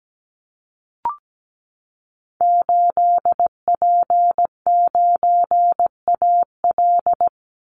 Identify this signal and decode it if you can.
Morse code, heard as a keyed tone, about 17 wpm, 700 Hz. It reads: 8P9AL
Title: Call Signs Repeated 3x - 17wpm